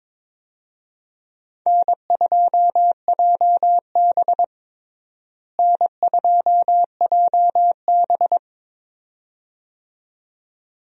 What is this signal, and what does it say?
Morse code, heard as a keyed tone, about 22 wpm, 700 Hz. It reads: N2JB N2JB
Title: USA Call Signs - 4 Characters - Repeated 2x - 22wpm